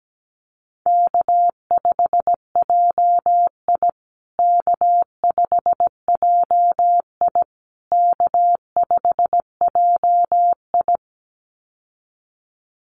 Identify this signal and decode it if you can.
Morse code, heard as a keyed tone, about 17 wpm, 700 Hz. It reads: K5JI K5JI K5JI
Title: USA Call Signs Repeated 3x 17wpm